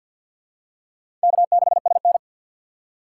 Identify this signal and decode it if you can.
Morse code, heard as a keyed tone, about 47 wpm, 700 Hz. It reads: K6SN
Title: USA Call Signs - 4 Characters - 2x Character Spacing - 50wpm